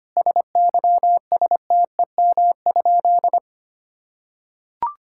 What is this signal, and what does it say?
Morse code, heard as a keyed tone, about 25 wpm, 700 Hz. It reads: SYSTEM?
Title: Sentences from Top 1000 Words - Review Part 6 - 25wpm